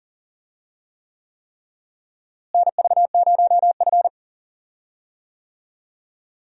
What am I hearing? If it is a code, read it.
Morse code, heard as a keyed tone, about 37 wpm, 700 Hz. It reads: NV0F
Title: USA Call Signs - 4 Characters - Repeated 3x - 40wpm